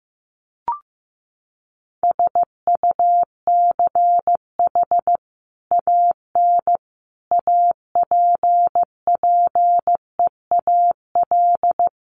Morse code, heard as a keyed tone, about 15 wpm, 700 Hz. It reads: SUCH AN APPEAL
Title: Sentences from Top 1300 Words 15wpm